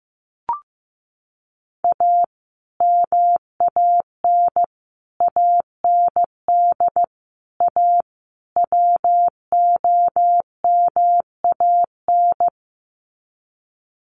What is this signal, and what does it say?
Morse code, heard as a keyed tone, about 15 wpm, 700 Hz. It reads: A MAN AND A WOMAN
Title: Sets of 5 Words Encore 15wpm